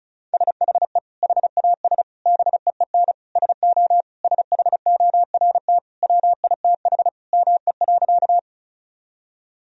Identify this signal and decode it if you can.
Morse code, heard as a keyed tone, about 36 wpm, 700 Hz. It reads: SHE HAS BEEN SO SHORT WITH ME.
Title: Sentences from Top 1000 Words - Review Part 2 - 35wpm